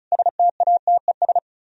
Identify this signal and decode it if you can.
Morse code, heard as a keyed tone, about 45 wpm, 700 Hz. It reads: STATES